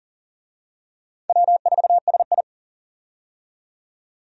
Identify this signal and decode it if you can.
Morse code, heard as a keyed tone, about 40 wpm, 700 Hz. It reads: W4SI
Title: USA Call Signs - 4 Characters - 40wpm